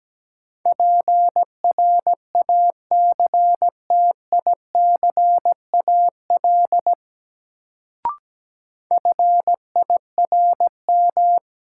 Morse code, heard as a keyed tone, about 17 wpm, 700 Hz. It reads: PRACTICAL FIRM
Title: Sentences from Top 1400 Words - New Word Drill Alt - 17wpm